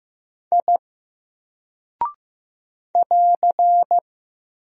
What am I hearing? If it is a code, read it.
Morse code, heard as a keyed tone, about 15 wpm, 700 Hz. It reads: I <AR>
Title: Single Character (Letter:Number:Punctuation:Prosign) 15wpm